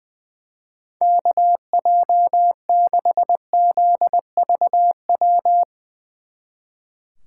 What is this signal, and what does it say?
Morse code, heard as a keyed tone, about 22 wpm, 700 Hz. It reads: KJ6ZVW
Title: Speed Racing - USA Call Signs 20wpm